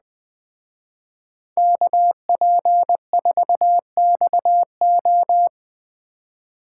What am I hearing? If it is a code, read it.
Morse code, heard as a keyed tone, about 20 wpm, 700 Hz. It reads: KP4XO
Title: Speed Racing - USA 2x2 Call Signs 17wpm